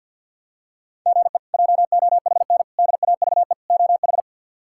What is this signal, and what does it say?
Morse code, heard as a keyed tone, about 46 wpm, 700 Hz. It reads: GE JOHN DAVE OH
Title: K1USN SST QSO Practice - 50wpm